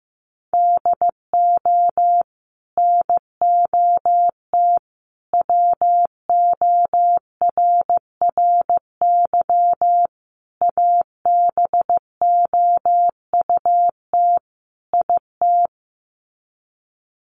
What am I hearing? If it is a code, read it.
Morse code, heard as a keyed tone, about 15 wpm, 700 Hz. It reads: DO NOT WORRY ABOUT IT
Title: Sets of 5 Words 15wpm